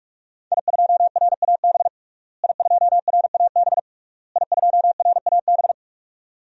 Morse code, heard as a keyed tone, about 44 wpm, 700 Hz. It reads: I2RAB I2RAB I2RAB
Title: DXCC-100 Call Signs - 5 Characters - Repeated 3x - 45wpm